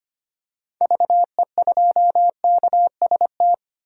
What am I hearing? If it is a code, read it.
Morse code, heard as a keyed tone, about 25 wpm, 700 Hz. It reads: VE2KST
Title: DXCC-100 Call Signs - 6 Characters - 25wpm